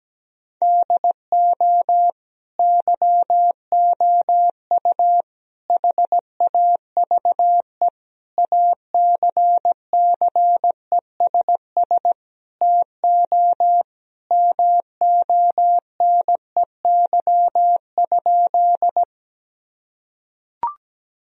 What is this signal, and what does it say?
Morse code, heard as a keyed tone, about 17 wpm, 700 Hz. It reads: DO YOU HAVE ACCESS TO MONEY?
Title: Sentences from Top 1000 Words - Review Part 4 - 17wpm